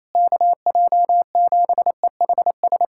Morse code, heard as a keyed tone, about 28 wpm, 700 Hz. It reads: KJ7EHS
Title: USA Callsign DE Callsign 28wpm